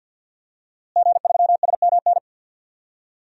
Morse code, heard as a keyed tone, about 52 wpm, 700 Hz. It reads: G3SMN
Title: DXCC-100 Call Signs - 5 Characters - Repeated 3x - 50wpm